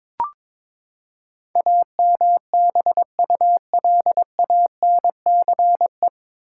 Morse code, heard as a keyed tone, about 22 wpm, 700 Hz. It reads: AMBULANCE
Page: Morse Code Ninja practice 22wpm